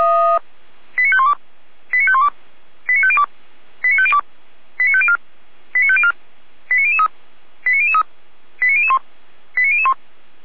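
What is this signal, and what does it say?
Five-tone selective call (ZVEI), heard as quick runs of five tones, tones of about 70 ms. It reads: 89512 89512 89502 89502 89504 89504 89003 89003 89001 89001
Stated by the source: ZVEI five-tone call 89099 received by SDR, from audio.f85360.feuerwehr-orlamuende.org fire brigade server.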